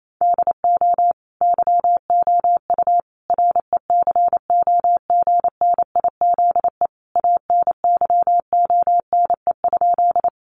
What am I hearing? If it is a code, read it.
Morse code, heard as a keyed tone, about 28 wpm, 700 Hz. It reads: DO YOU RECOGNIZE ANYONE?
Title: Sentences from Top 1400 Words 28wpm